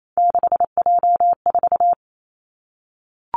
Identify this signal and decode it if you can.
Morse code, heard as a keyed tone, about 28 wpm, 700 Hz. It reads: 6J4
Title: Three Challenging Letter-Number Combinations 28wpm